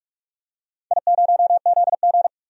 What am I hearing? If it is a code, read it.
Morse code, heard as a keyed tone, about 46 wpm, 700 Hz. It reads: I0ZG
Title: DXCC-100 Call Signs - 4 Characters - 45wpm